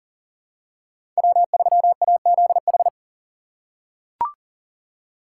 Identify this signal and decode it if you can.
Morse code, heard as a keyed tone, about 37 wpm, 700 Hz. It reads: W3AZH
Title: USA Call Signs - 5 Characters - 2x Character Spacing - 40wpm